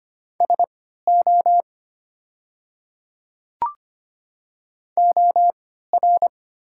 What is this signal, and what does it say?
Morse code, heard as a keyed tone, about 25 wpm, spaced as for 14 wpm, 700 Hz. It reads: SO OR
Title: 3x Character Spacing - Top 100 Words - 25wpm